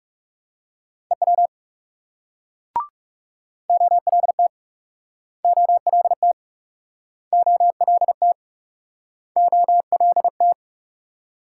Morse code, heard as a keyed tone, about 41 wpm, 700 Hz. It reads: EW OLT OLT OLT OLT